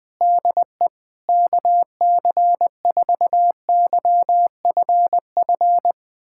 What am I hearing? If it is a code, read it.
Morse code, heard as a keyed tone, about 20 wpm, 700 Hz. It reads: DE KC4YFF
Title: USA Callsign DE Callsign - 6 Characters - 20wpm